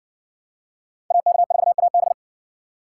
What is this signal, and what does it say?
Morse code, heard as a keyed tone, about 55 wpm, 700 Hz. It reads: AK4AB